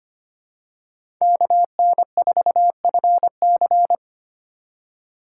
Morse code, heard as a keyed tone, about 25 wpm, 700 Hz. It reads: KN4FC
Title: USA Call Signs - 5 Characters - 4x Character Spacing - 25wpm